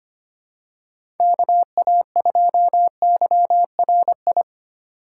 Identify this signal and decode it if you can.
Morse code, heard as a keyed tone, about 25 wpm, 700 Hz. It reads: KA2YRI
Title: USA Call Signs - 6 Characters - 3x Character Spacing - 25wpm